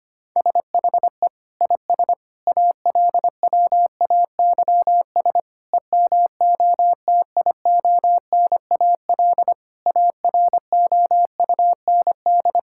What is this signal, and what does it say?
Morse code, heard as a keyed tone, about 25 wpm, 700 Hz. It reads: SHE IS ALWAYS EMOTIONAL AROUND
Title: Sentences from Top 1000 Words - Review Part 5 - 25wpm